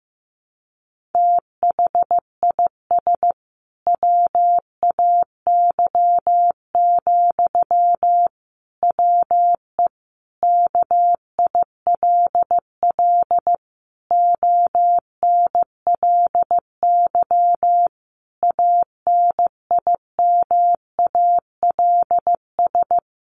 Morse code, heard as a keyed tone, about 15 wpm, 700 Hz. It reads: THIS WAY, WE KILL ONLY ANIMALS
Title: A Sound Of Thunder 15wpm